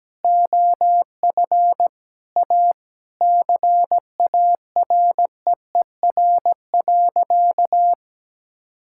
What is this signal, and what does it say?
Morse code, heard as a keyed tone, about 17 wpm, 700 Hz. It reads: OF A CAREER.
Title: Sentences from Top 1000 Words - Review Part 6 - 17wpm